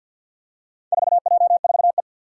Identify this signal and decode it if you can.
Morse code, heard as a keyed tone, about 53 wpm, 700 Hz. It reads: 4J4E
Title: Four Challenging Letter-Number Combinations 50wpm